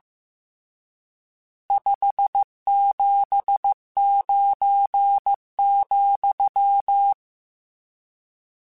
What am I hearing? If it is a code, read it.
Morse code, heard as a keyed tone, about 15 wpm, 800 Hz. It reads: 579,
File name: RSTs / RST-mp3s / Various RST @13